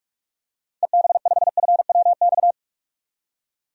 Morse code, heard as a keyed tone, about 42 wpm, 700 Hz. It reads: EB5FWX
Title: DXCC-100 Call Signs - 6 Characters - 45wpm